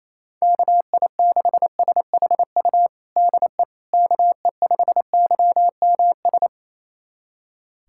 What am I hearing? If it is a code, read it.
Morse code, heard as a keyed tone, about 28 wpm, 700 Hz. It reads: KI6SHU DE KE5YMS
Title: USA Callsign DE Callsign - 6 Characters - 28wpm